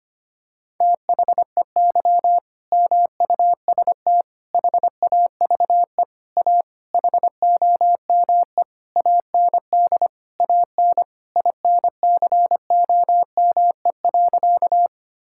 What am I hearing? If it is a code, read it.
Morse code, heard as a keyed tone, about 25 wpm, 700 Hz. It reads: THEY MUST HAVE A HOME AND AN INCOME.
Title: Sentences from Top 1000 Words - Review Part 5 - 25wpm